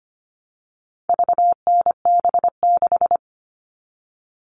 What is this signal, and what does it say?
Morse code, heard as a keyed tone, about 25 wpm, 700 Hz. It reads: VNB6
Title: All Cut Numbers - 4-Digit 25wpm